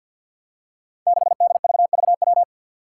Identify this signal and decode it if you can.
Morse code, heard as a keyed tone, about 48 wpm, 700 Hz. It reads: 6DVVW
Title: Five Challenging Letter-Number Combinations 50wpm